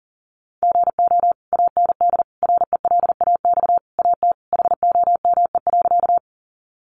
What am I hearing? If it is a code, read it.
Morse code, heard as a keyed tone, about 40 wpm, 700 Hz. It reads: GO AND RELAX AT HOME.
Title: Sentences from Top 1400 Words 40wpm